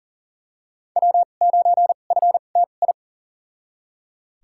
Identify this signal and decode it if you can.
Morse code, heard as a keyed tone, about 40 wpm, spaced as for 29 wpm, 700 Hz. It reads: W9FTI
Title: USA Call Signs - 5 Characters - 2x Character Spacing - 40wpm